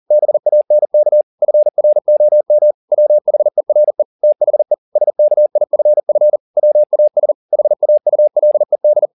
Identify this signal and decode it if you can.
Morse code, heard as a keyed tone, about 40 wpm, 600 Hz. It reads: BANK FROM WHERE THE SKIFF WAS HAULED
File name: the old man and the sea 1 40wpm